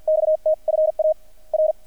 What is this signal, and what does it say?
Morse code, heard as a keyed tone, about 50 wpm, spaced as for 44 wpm, 620 Hz, with radio noise. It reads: XT FA R
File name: archived / CRSnet